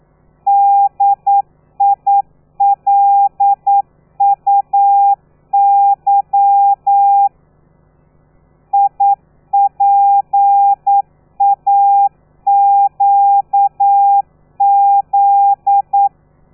Morse code, heard as a keyed tone, about 9 wpm, 800 Hz. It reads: DILUY IPAQZ
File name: telegrafia / i2tfj / Lez13 40